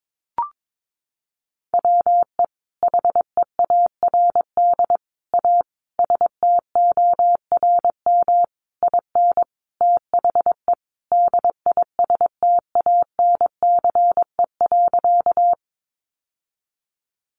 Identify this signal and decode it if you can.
Morse code, heard as a keyed tone, about 22 wpm, 700 Hz. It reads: WE HEARD A STORM IN THE DISTANCE.
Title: Sentences from Top 1500 Words 22wpm